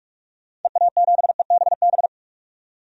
Morse code, heard as a keyed tone, about 45 wpm, 700 Hz. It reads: EA7EBB
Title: DXCC-100 Call Signs - 6 Characters - 45wpm